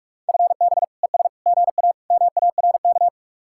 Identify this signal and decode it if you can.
Morse code, heard as a keyed tone, about 44 wpm, 700 Hz. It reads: FB ES GA MARK